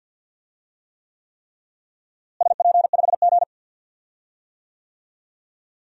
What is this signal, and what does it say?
Morse code, heard as a keyed tone, about 53 wpm, 700 Hz. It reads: SP5G
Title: DXCC-100 Call Signs - 4 Characters - Repeated 3x - 50wpm